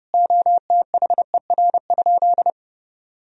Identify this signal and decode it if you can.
Morse code, heard as a keyed tone, about 30 wpm, 700 Hz. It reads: OTHER?
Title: Sentences from Top 200 Words and Abbreviations 30wpm